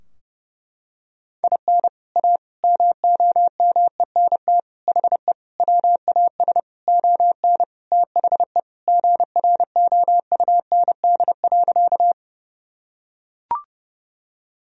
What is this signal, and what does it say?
Morse code, heard as a keyed tone, about 30 wpm, 700 Hz. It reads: IN A MOMENT HE WAS ON THE GROUND.
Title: Sentences from Top 1000 Words - Review Part 2 - 30wpm